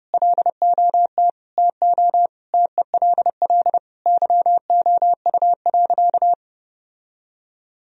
Morse code, heard as a keyed tone, about 32 wpm, 700 Hz. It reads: LOT TO TELL YOU.
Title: Sentences from Top 1000 Words - Review Part 3 - 30wpm